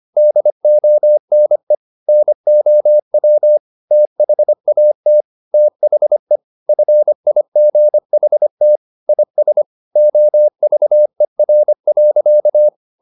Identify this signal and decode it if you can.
Morse code, heard as a keyed tone, about 25 wpm, 600 Hz. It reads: DONE NOW THAT THE FIGHT IS OVER.